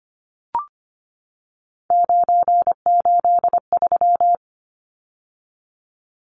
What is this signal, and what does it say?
Morse code, heard as a keyed tone, about 25 wpm, 690 Hz. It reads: 983
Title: Numbers - 3-Digit 25wpm